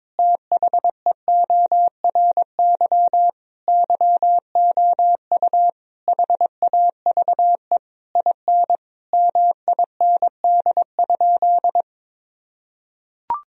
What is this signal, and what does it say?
Morse code, heard as a keyed tone, about 22 wpm, 700 Hz. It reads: THEORY YOU HAVE IN MIND?
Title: Sentences from Top 1000 Words - Review Part 3 - 22wpm